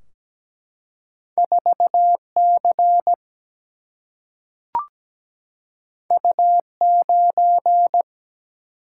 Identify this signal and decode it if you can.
Morse code, heard as a keyed tone, about 17 wpm, 700 Hz. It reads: 4C U9
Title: Single-Letter Single-Number 17wpm